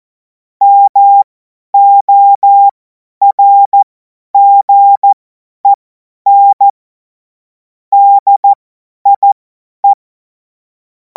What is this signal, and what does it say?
Morse code, heard as a keyed tone, about 14 wpm, spaced as for 10 wpm, 800 Hz. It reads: MORGEN DIE
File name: Morsekurs / Witze / Witz 10. März 2026 50 F70